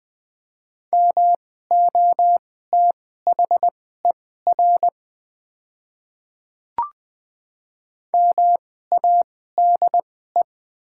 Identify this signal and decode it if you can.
Morse code, heard as a keyed tone, about 20 wpm, spaced as for 15 wpm, 690 Hz. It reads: MOTHER MADE
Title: 2x Character Spacing - Top 200 Words - 20wpm